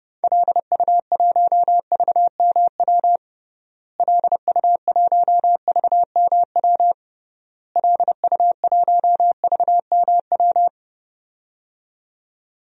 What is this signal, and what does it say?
Morse code, heard as a keyed tone, about 30 wpm, 700 Hz. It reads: LU1VMW LU1VMW LU1VMW
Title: DXCC-100 Call Signs - 6 Characters - Repeated 3x - 30wpm